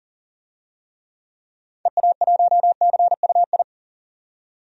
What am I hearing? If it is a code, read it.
Morse code, heard as a keyed tone, about 40 wpm, 700 Hz. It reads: EA1CUI
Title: DXCC-100 Call Signs - 6 Characters - 40wpm